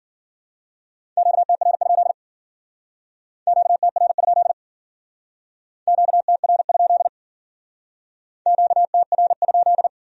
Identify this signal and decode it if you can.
Morse code, heard as a keyed tone, about 48 wpm, 700 Hz. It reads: QTR? QTR? QTR? QTR?
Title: Speed-Racing Q-Codes - 40wpm